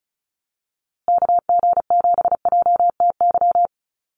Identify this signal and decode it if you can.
Morse code, heard as a keyed tone, about 34 wpm, 700 Hz. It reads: KG7JTY
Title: USA Call Signs Repeated 2x 35wpm